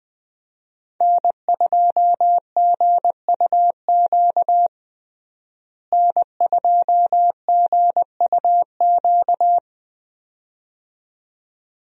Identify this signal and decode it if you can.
Morse code, heard as a keyed tone, about 21 wpm, 700 Hz. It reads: N2GUQ N2GUQ
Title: USA Call Signs - 5 Characters - Repeated 2x - 20wpm